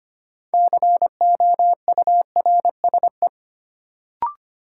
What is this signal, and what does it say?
Morse code, heard as a keyed tone, about 25 wpm, 700 Hz. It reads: COURSE
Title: Sentences from Top 500 Words - New Word Drill - 25wpm